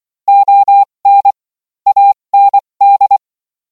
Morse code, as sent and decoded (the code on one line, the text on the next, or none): --- -. / .- -. -..
ON AND